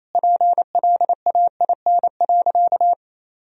Morse code, heard as a keyed tone, about 27 wpm, 700 Hz. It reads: PLAIN.
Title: Sentences from Top 1000 Words - Review Part 2 - 28wpm